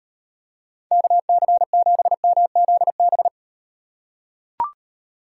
Morse code, heard as a keyed tone, about 38 wpm, 700 Hz. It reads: KC7MZB